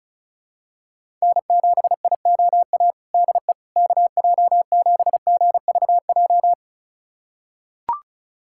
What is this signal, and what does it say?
Morse code, heard as a keyed tone, about 36 wpm, 690 Hz. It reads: N7IOA DE KJ7GVJ
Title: USA Callsign DE Callsign 35wpm